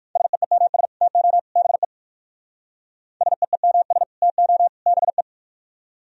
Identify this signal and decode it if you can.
Morse code, heard as a keyed tone, about 59 wpm, 700 Hz. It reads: SEEMS TO BE SEEMS TO BE